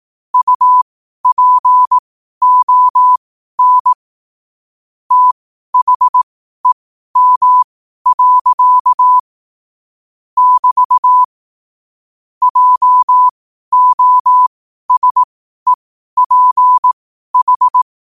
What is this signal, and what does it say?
Morse code, heard as a keyed tone, about 18 wpm, spaced as for 12 wpm, 1000 Hz. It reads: UPON THEM. <BT> JOSEPH